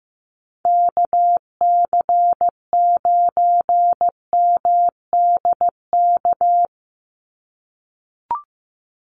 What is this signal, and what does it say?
Morse code, heard as a keyed tone, about 15 wpm, 700 Hz. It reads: KC9MDK
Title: USA Call Signs 15wpm